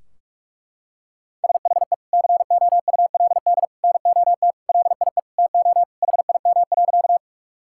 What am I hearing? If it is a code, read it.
Morse code, heard as a keyed tone, about 46 wpm, 700 Hz. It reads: SHE COULD NOT LIE TO HIM.
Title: Sentences from Top 1600 Words 45wpm